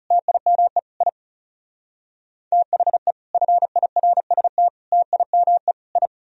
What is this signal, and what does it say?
Morse code, heard as a keyed tone, about 35 wpm, 700 Hz. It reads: TIME I THE FIRST TIME I
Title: Speed-Racing - Sets of 4 Words 30wpm